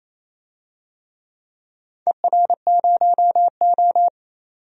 Morse code, heard as a keyed tone, about 28 wpm, 700 Hz. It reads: ER0O
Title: DXCC-100 Call Signs - 4 Characters - 28wpm